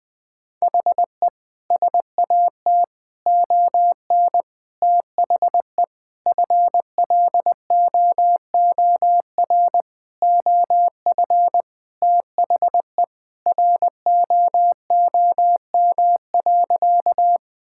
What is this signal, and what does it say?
Morse code, heard as a keyed tone, about 20 wpm, 700 Hz. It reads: HE SAT ON THE FLOOR OF THE ROOM.
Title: Sentences with 5-Word Phrases - Spoken Beginning - 20wpm